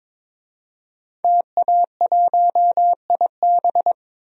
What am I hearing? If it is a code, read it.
Morse code, heard as a keyed tone, about 22 wpm, 700 Hz. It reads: TA1IB
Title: DXCC-100 Call Signs - 5 Characters - 22wpm